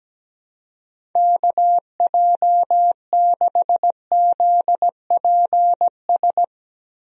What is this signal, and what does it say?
Morse code, heard as a keyed tone, about 17 wpm, 700 Hz. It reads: KJ6ZPS
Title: USA Call Signs - 6 Characters - Repeated 2x - 17wpm